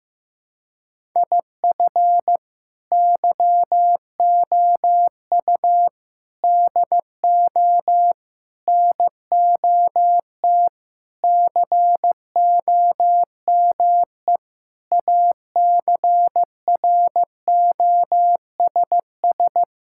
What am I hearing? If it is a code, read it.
Morse code, heard as a keyed tone, about 16 wpm, 700 Hz. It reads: IF YOU DO NOT COME ACROSS
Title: Sentences from Top 1000 Words - Review Part 5 - 15wpm